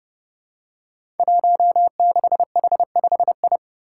Morse code, heard as a keyed tone, about 29 wpm, 700 Hz. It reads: 16H5I
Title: Five Challenging Letter-Number Combinations 30wpm